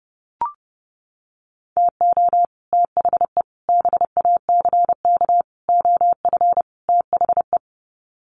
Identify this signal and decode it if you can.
Morse code, heard as a keyed tone, about 28 wpm, 700 Hz. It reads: TO THE BACK OF THE
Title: Sets of 5 Words 30wpm